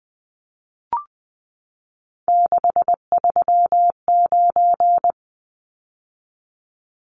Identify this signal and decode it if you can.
Morse code, heard as a keyed tone, about 20 wpm, 700 Hz. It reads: 639
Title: Numbers - 3-Digit 20wpm